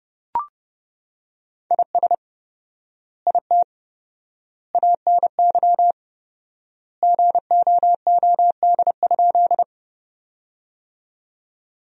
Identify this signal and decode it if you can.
Morse code, heard as a keyed tone, about 30 wpm, 700 Hz. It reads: IS IT ANY GOOD?
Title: 4x Word-Spacing - Sentences from Top 200 Words - 30wpm